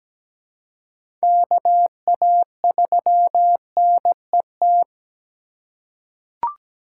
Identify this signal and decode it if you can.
Morse code, heard as a keyed tone, about 17 wpm, 700 Hz. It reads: KA3NET
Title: USA Call Signs - 6 Characters - 3x Character Spacing - 17wpm